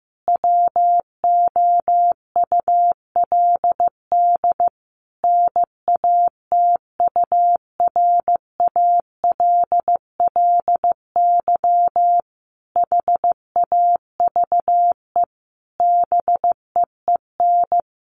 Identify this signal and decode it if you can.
Morse code, heard as a keyed tone, about 15 wpm, 700 Hz. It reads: WOULD NATURALLY HAVE BEEN
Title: Sentences from Top 1300 Words 15wpm